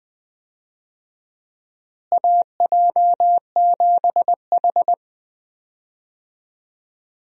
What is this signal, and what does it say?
Morse code, heard as a keyed tone, about 20 wpm, 700 Hz. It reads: AJ7H
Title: USA Call Signs - 4 Characters - Repeated 2x - 20wpm